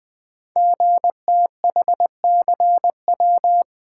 Morse code, heard as a keyed tone, about 20 wpm, 700 Hz. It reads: GTHCW
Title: Code Groups - 5 Letters - 20wpm